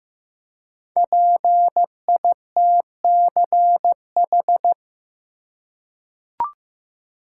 Morse code, heard as a keyed tone, about 15 wpm, 700 Hz. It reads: PITCH
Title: Five Letter Words - 15wpm